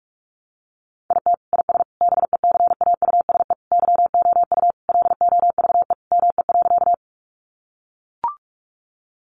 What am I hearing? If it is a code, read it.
Morse code, heard as a keyed tone, about 44 wpm, 700 Hz. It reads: IT IS BECAUSE YOU LOVE ME.
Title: Sentences from Top 600 Words 45wpm